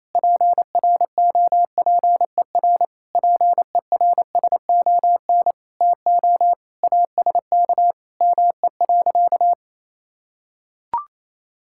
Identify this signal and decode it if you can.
Morse code, heard as a keyed tone, about 28 wpm, 700 Hz. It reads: PROPER PERSON TO ASK ME.
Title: Sentences from Top 1000 Words - Review Part 5 - 28wpm